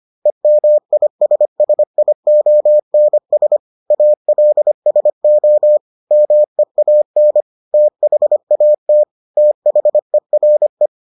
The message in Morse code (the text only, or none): EMISSIONS ALSO MEAN THAT THERE